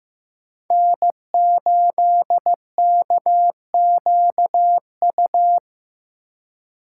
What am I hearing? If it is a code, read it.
Morse code, heard as a keyed tone, about 15 wpm, 700 Hz. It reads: N8KQU
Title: USA Call Signs - 5 Characters - 4x Character Spacing - 15wpm